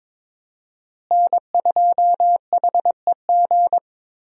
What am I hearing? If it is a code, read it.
Morse code, heard as a keyed tone, about 21 wpm, 700 Hz. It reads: N2HEG
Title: USA Call Signs - 5 Characters - Repeated 3x - 22wpm